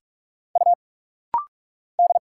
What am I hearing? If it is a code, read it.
Morse code, heard as a keyed tone, about 44 wpm, 700 Hz. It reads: U D
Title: Single Letter-Number Alt - Rapid-Fire 45wpm